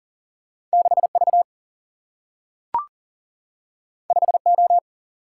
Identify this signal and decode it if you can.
Morse code, heard as a keyed tone, about 40 wpm, 700 Hz. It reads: 6V 5O
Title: Single-Letter Single-Number 40wpm